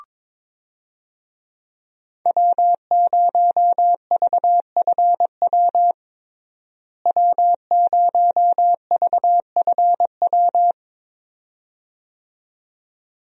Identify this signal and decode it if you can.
Morse code, heard as a keyed tone, about 22 wpm, 700 Hz. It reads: W0VFW W0VFW
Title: USA Call Signs - 5 Characters - Repeated 2x - 22wpm